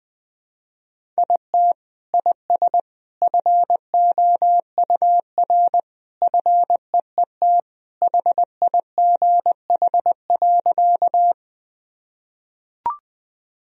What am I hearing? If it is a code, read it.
Morse code, heard as a keyed tone, about 20 wpm, 700 Hz. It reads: IT IS FOUR FEET HIGH.
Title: Sentences from Top 1000 Words - Review Part 1 - 20wpm